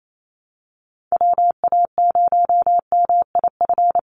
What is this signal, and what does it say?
Morse code, heard as a keyed tone, about 28 wpm, 700 Hz. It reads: WA0MIF
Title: USA Call Signs 28wpm